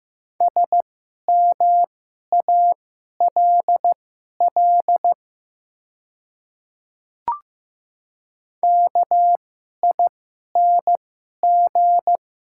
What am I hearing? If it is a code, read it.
Morse code, heard as a keyed tone, about 15 wpm, spaced as for 11 wpm, 700 Hz. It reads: SMALL KING
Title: 2x Character Spacing - Top 500 Words - 15wpm